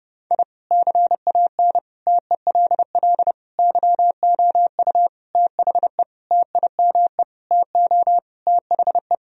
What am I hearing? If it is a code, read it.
Morse code, heard as a keyed tone, about 30 wpm, 700 Hz. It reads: I CAN TELL YOU THE TIME TO THE
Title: Sentences from Top 1000 Words - Review Part 1 - 30wpm